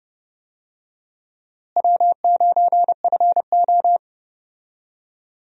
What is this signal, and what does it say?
Morse code, heard as a keyed tone, about 32 wpm, 700 Hz. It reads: W9FO
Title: USA Call Signs - 4 Characters - Repeated 3x - 30wpm